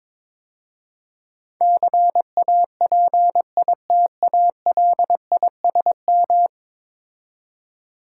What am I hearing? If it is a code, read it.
Morse code, heard as a keyed tone, about 22 wpm, 700 Hz. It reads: CAPITALISM
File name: practice / 22wpm / 10 Letter Words - First 3 Letters Given - 22wpm